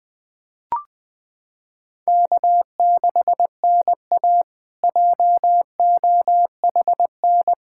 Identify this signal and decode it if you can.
Morse code, heard as a keyed tone, about 20 wpm, 700 Hz. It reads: K6NA JOHN
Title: K1USN SST QSO Practice - 20wpm